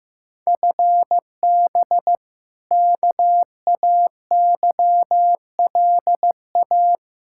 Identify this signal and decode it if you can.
Morse code, heard as a keyed tone, about 15 wpm, 700 Hz. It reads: FB KAYLA
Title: K1USN SST QSO Practice 15wpm